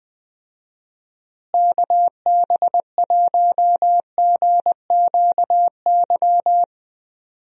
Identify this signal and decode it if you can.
Morse code, heard as a keyed tone, about 21 wpm, 700 Hz. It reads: KB1GQY
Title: USA Call Signs - 6 Characters - Repeated 3x - 20wpm